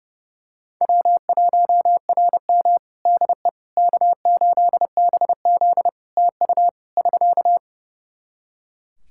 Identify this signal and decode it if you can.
Morse code, heard as a keyed tone, about 30 wpm, 700 Hz. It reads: W1RM DE K8BZ TU <SK>
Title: Practice QSO - Exchange 4 - Ending - 30wpm